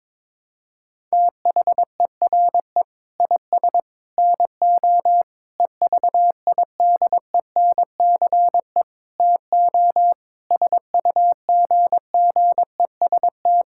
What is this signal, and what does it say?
Morse code, heard as a keyed tone, about 22 wpm, 700 Hz. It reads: THERE IS NO EVIDENCE TO SUGGEST
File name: practice / 22wpm / Sets of 7 Words - 22wpm